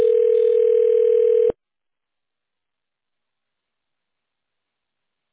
Ringback tone (North America), 440 + 480 Hz.